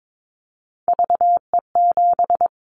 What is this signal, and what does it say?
Morse code, heard as a keyed tone, about 22 wpm, 700 Hz. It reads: VE7
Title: Callsign Prefixes for Top 100 DXCC - 22wpm